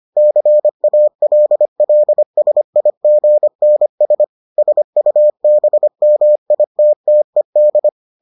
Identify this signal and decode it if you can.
Morse code, heard as a keyed tone, about 25 wpm, 600 Hz. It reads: CALLSIGNS SUBMITTED